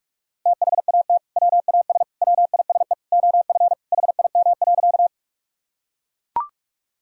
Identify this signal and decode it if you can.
Morse code, heard as a keyed tone, about 41 wpm, 700 Hz. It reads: THAT WAS WISE OF HIM.
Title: Sentences from Top 1600 Words 45wpm